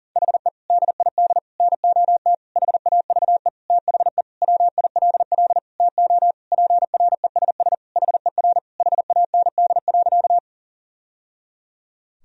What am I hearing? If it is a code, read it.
Morse code, heard as a keyed tone, about 40 wpm, 700 Hz. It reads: HE DID NOT HAVE THE WILL TO PRESS HER HAND.
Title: Sentences from Top 1000 Words - Review Part 1 - 40wpm